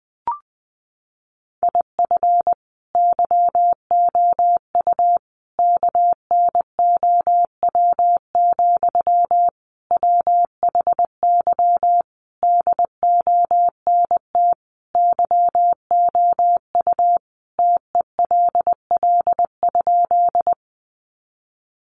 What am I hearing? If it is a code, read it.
Morse code, heard as a keyed tone, about 20 wpm, 700 Hz. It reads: IF YOU KNOW, WHY DONT YOU TELL?
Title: Sentences from Top 300 Words 20wpm